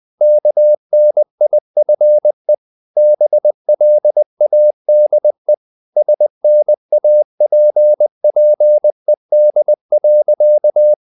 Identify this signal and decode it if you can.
Morse code, heard as a keyed tone, about 20 wpm, 600 Hz. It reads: KNIFE BLADE SNAPPED.
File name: the old man and the sea part 2 20wpm